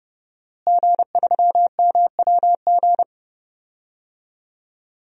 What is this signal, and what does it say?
Morse code, heard as a keyed tone, about 30 wpm, 700 Hz. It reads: G3MWG
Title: DXCC-100 Call Signs - 5 Characters - Repeated 3x - 30wpm